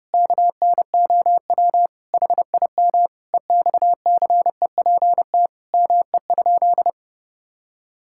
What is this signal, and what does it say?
Morse code, heard as a keyed tone, about 32 wpm, 700 Hz. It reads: KNOW HIM EXCEPT ME?
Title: Sentences from Top 1600 Words 30wpm